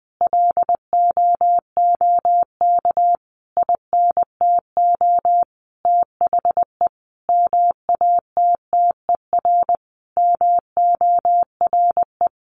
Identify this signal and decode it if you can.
Morse code, heard as a keyed tone, about 19 wpm, 700 Hz. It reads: LOOK INTO THE MATTER MORE
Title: Sentences from Top 1400 Words 20wpm